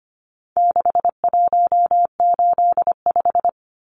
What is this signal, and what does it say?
Morse code, heard as a keyed tone, about 25 wpm, 700 Hz. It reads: 6185
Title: Numbers - 4-Digit 25wpm